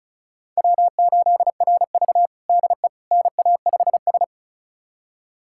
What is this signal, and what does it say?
Morse code, heard as a keyed tone, about 36 wpm, 700 Hz. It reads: W8RV DE NA5S
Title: USA Callsign DE Callsign - 4 Characters - 35wpm